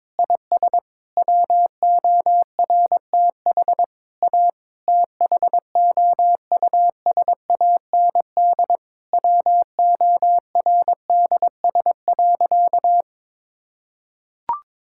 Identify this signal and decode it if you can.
Morse code, heard as a keyed tone, about 22 wpm, 700 Hz. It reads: IS WORTH A THOUSAND WORDS.
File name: Sentences with 7-Word Phrases - 22wpm